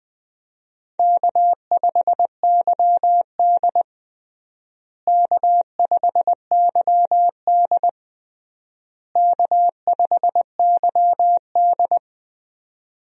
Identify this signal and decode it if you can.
Morse code, heard as a keyed tone, about 21 wpm, 700 Hz. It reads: K5YD K5YD K5YD
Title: USA Call Signs - 4 Characters - Repeated 3x - 20wpm